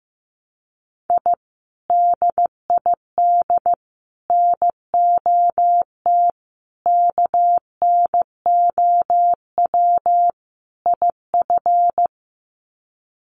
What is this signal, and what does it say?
Morse code, heard as a keyed tone, about 15 wpm, 700 Hz. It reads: I DID NOT KNOW IF
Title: Sets of 5 Words Encore 15wpm